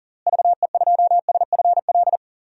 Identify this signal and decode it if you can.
Morse code, heard as a keyed tone, about 40 wpm, 700 Hz. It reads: VE2SFL
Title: DXCC-100 Call Signs - 6 Characters - Repeated 3x - 40wpm